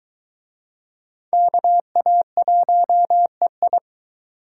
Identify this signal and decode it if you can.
Morse code, heard as a keyed tone, about 23 wpm, 700 Hz. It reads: KA1EI